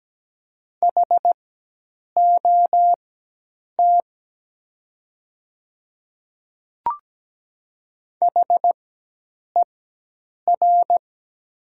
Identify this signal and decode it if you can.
Morse code, heard as a keyed tone, about 17 wpm, spaced as for 8 wpm, 700 Hz. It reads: HOT HER
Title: 4x Character Spacing - Top 100 Words - 17wpm